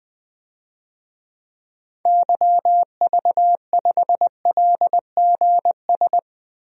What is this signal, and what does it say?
Morse code, heard as a keyed tone, about 21 wpm, 700 Hz. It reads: YV5LGS
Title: DXCC-100 Call Signs - 6 Characters - 20wpm